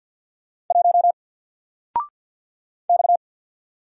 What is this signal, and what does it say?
Morse code, heard as a keyed tone, about 50 wpm, 700 Hz. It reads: 1 X
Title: Single Character (Letter:Number:Punctuation:Prosign) 50wpm